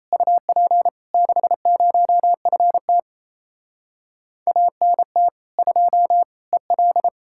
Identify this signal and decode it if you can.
Morse code, heard as a keyed tone, about 38 wpm, 700 Hz. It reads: UP 60FT ANT 2 EL